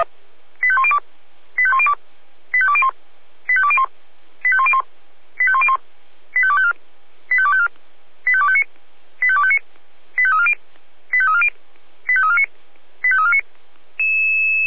Five-tone selective call (ZVEI), heard as quick runs of five tones, tones of about 70 ms. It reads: 85102 85102 85201 85201 85101 85101 85254 85254 85259 85259 85340 85340 85349 85349